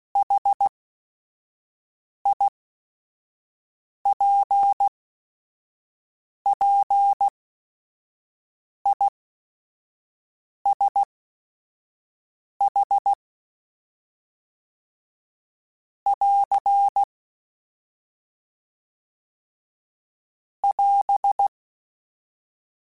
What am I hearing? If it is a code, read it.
Morse code, heard as a keyed tone, about 16 wpm, spaced as for 5 wpm, 800 Hz. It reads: HIPPISH <AR> <AS>